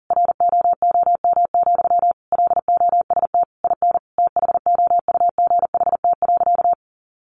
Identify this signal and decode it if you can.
Morse code, heard as a keyed tone, about 40 wpm, 700 Hz. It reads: ROOM, LOST IN THOUGHT.
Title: Sentences from Top 1400 Words 40wpm